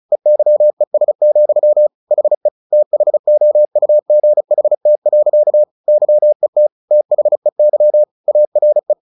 Morse code, heard as a keyed tone, about 34 wpm, 600 Hz. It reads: EYES, HE THOUGHT. YET THEY ARE
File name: the old man and the sea / the old man and the sea 1 35wpm